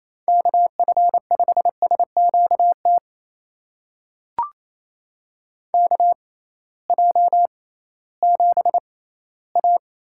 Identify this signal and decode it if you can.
Morse code, heard as a keyed tone, about 28 wpm, spaced as for 16 wpm, 700 Hz. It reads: KF5SQT KJ7A
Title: USA Call Signs - 6 Characters - 6x Character Spacing - 28wpm